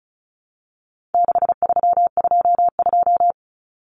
Morse code, heard as a keyed tone, about 34 wpm, 700 Hz. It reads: 6322
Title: Numbers - 4-Digit 35wpm